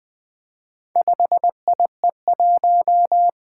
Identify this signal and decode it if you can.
Morse code, heard as a keyed tone, about 20 wpm, 700 Hz. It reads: 5IE1
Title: Four Letter-Number Combinations - 20wpm